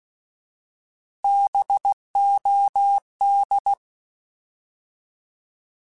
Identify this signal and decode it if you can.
Morse code, heard as a keyed tone, about 16 wpm, 800 Hz. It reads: BOD
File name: Alphabetical / Alphabetical mp3s / Bs @15